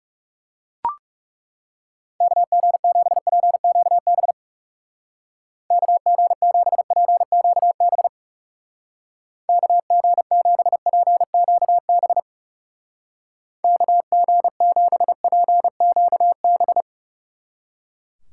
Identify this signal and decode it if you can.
Morse code, heard as a keyed tone, about 39 wpm, 700 Hz. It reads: KG7PQB KG7PQB KG7PQB KG7PQB